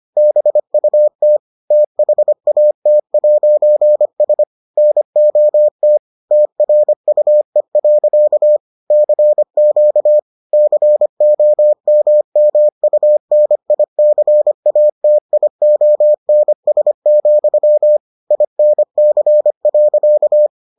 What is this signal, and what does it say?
Morse code, heard as a keyed tone, about 25 wpm, 600 Hz. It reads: BUT THAT'S NOT TRUE. CQ COMMUNICATIONS, INC.